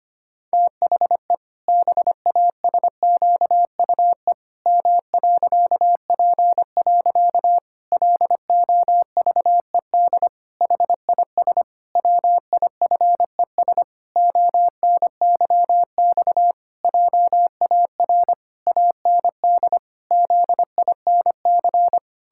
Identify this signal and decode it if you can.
Morse code, heard as a keyed tone, about 25 wpm, 700 Hz. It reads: THE BASQUE M.P. LOVED HIS WIFES ONYX JAR AND ZINC
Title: Pangram Challenge v2 - 25wpm